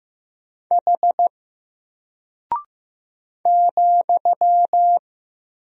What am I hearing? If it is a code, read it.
Morse code, heard as a keyed tone, about 15 wpm, 700 Hz. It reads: H ,
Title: Single Character (Letter:Number:Punctuation:Prosign) Alt 15wpm